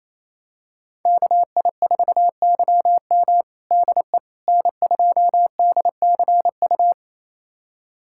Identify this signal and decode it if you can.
Morse code, heard as a keyed tone, about 28 wpm, 700 Hz. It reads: KI4YM DE N2DCU
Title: USA Callsign DE Callsign - 5 Characters - 28wpm